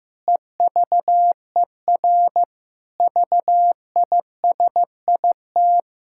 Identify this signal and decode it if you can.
Morse code, heard as a keyed tone, about 15 wpm, 700 Hz. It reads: EVER VISIT
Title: Sentences from Top 1000 Words - Review Part 6 - 15wpm